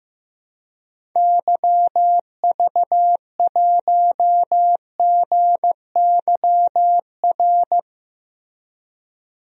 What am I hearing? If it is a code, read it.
Morse code, heard as a keyed tone, about 15 wpm, 700 Hz. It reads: YV1GYR
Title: DXCC-100 Call Signs - 6 Characters - Repeated 2x - 15wpm